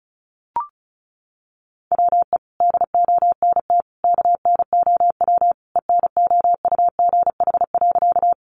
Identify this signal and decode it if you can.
Morse code, heard as a keyed tone, about 34 wpm, 700 Hz. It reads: WE DONT KNOW ENOUGH.
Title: Sentences from Top 400 Words 35wpm